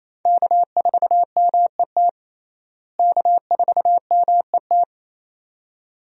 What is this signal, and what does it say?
Morse code, heard as a keyed tone, about 28 wpm, 700 Hz. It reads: K4MET K4MET
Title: USA Call Signs - 5 Characters - Repeated 2x - 28wpm